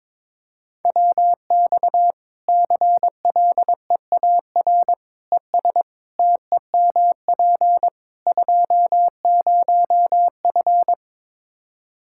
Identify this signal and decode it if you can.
Morse code, heard as a keyed tone, about 22 wpm, 700 Hz. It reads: WX CLEAR ES TEMP 20F
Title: Practice QSO - Exchange 2 - Info - 22wpm